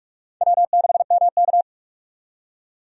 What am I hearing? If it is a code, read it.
Morse code, heard as a keyed tone, about 46 wpm, 700 Hz. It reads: W6MK
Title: USA Call Signs - 4 Characters - 45wpm